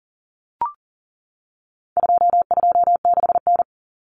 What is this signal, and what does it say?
Morse code, heard as a keyed tone, about 40 wpm, 700 Hz. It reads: 226N
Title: Common Cut Numbers - 4-Digit 40wpm